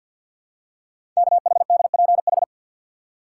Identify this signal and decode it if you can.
Morse code, heard as a keyed tone, about 48 wpm, 690 Hz. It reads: KHDPH
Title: Code Groups - 5 Letters - 50wpm